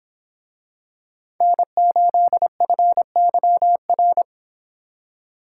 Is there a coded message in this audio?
Yes: Morse code from 1.4 to 4.2 s, heard as a keyed tone, about 24 wpm, 700 Hz. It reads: N8FYR